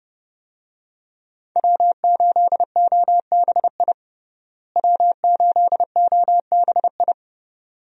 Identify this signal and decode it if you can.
Morse code, heard as a keyed tone, about 30 wpm, 700 Hz. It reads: W8OBI W8OBI
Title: USA Call Signs - 5 Characters - Repeated 2x - 30wpm